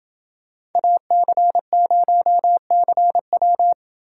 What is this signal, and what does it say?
Morse code, heard as a keyed tone, about 28 wpm, 700 Hz. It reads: AC0CW